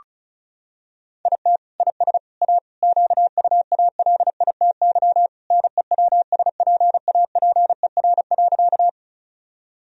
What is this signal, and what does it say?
Morse code, heard as a keyed tone, about 34 wpm, 700 Hz. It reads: IT IS A QUALITY NEWSPAPER.
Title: Sentences from Top 1000 Words - Review Part 4 - 35wpm